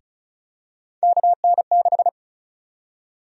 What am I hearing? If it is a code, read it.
Morse code, heard as a keyed tone, about 36 wpm, 700 Hz. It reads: KN6
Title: USA Call Sign Prefixes - 35wpm